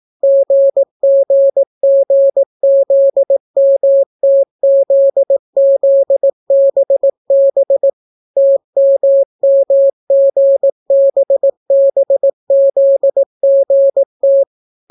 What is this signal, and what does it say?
Morse code, heard as a keyed tone, about 18 wpm, 550 Hz. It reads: GGGZMTZZBB TMMGBBZGT